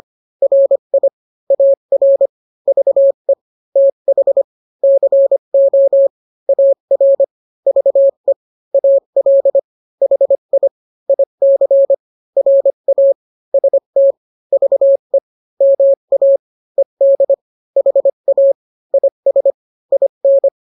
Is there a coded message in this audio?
Yes: Morse code from 0.4 to 20.5 s, heard as a keyed tone, about 25 wpm, 550 Hz. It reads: RI AR VE TH CO AR VE AL HI IC RA ST VE MA ED HA IS IN